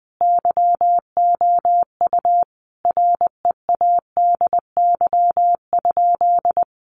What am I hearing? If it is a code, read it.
Morse code, heard as a keyed tone, about 20 wpm, 700 Hz. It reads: YOU READY?
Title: Sentences from Top 400 Words 20wpm